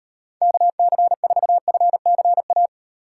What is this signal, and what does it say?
Morse code, heard as a keyed tone, about 36 wpm, 700 Hz. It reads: KC4FCA